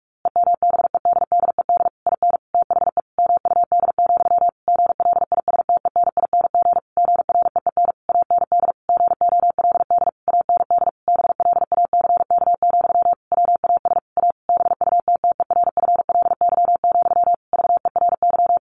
Morse code, heard as a keyed tone, about 44 wpm, 700 Hz. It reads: EMBEDDED IN THE MUD, GLISTENING GREEN AND GOLD AND BLACK, WAS A BUTTERFLY, VERY
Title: A Sound Of Thunder 45wpm